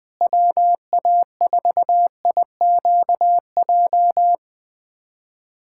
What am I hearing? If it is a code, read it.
Morse code, heard as a keyed tone, about 21 wpm, 700 Hz. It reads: WA4IQJ
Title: USA Callsign DE Callsign - 6 Characters - 20wpm